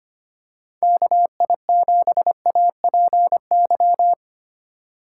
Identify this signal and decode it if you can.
Morse code, heard as a keyed tone, about 25 wpm, 700 Hz. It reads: KI7APY
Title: USA Call Signs - 2x Character Spacing - 25wpm